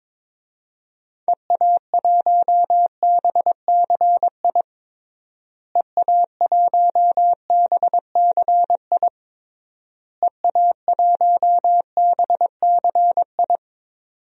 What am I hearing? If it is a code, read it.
Morse code, heard as a keyed tone, about 22 wpm, 700 Hz. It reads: EA1BCI EA1BCI EA1BCI
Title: DXCC-100 Call Signs - 6 Characters - Repeated 3x - 22wpm